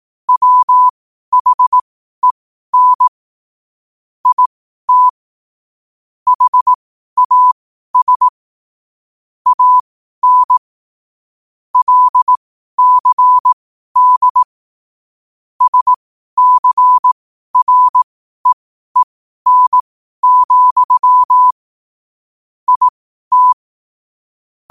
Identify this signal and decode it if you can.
Morse code, heard as a keyed tone, about 18 wpm, spaced as for 12 wpm, 1000 Hz. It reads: WHEN IT HAS AN LCD SCREEN, IT
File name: qotdfiles / 12wpm